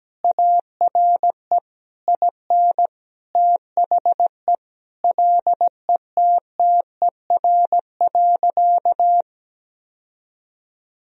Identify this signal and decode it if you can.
Morse code, heard as a keyed tone, about 17 wpm, 700 Hz. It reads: ARE IN THE LETTER.
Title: Sentences from Top 1000 Words - Review Part 4 - 17wpm